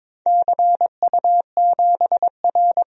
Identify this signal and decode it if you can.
Morse code, heard as a keyed tone, about 21 wpm, 700 Hz. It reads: CU7R
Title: DXCC-100 Call Signs - 4 Characters - 22wpm